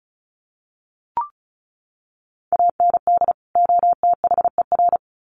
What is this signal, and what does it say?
Morse code, heard as a keyed tone, about 34 wpm, 700 Hz. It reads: AND OTHER
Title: Sets of 2 Words 35wpm